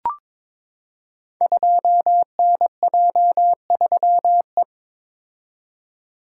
Morse code, heard as a keyed tone, about 22 wpm, 700 Hz. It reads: 2NJ3E
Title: Code Groups - 5 Letter:Numbers - 22wpm